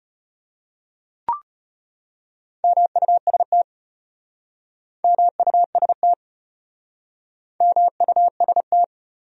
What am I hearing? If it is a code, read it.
Morse code, heard as a keyed tone, about 34 wpm, 700 Hz. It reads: MUST MUST MUST